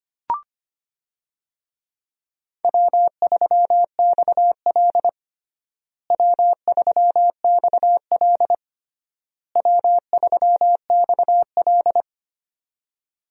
Morse code, heard as a keyed tone, about 25 wpm, 700 Hz. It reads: W3XL W3XL W3XL
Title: USA Call Signs - 4 Characters - Repeated 3x - 25wpm